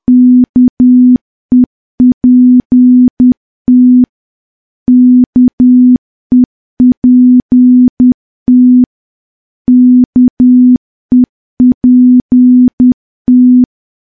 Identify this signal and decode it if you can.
Morse code, heard as a keyed tone, about 10 wpm, 250 Hz. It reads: KEPT KEPT KEPT
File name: morse / morse-training-10WPM